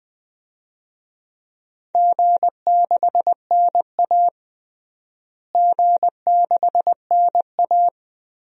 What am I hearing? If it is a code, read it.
Morse code, heard as a keyed tone, about 20 wpm, 700 Hz. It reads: G6NA G6NA
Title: DXCC-100 Call Signs - 4 Characters - Repeated 2x - 20wpm